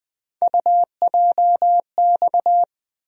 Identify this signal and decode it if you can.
Morse code, heard as a keyed tone, about 20 wpm, 700 Hz. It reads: UJX
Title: USA Call Sign Postfixes - 3 Letters 20wpm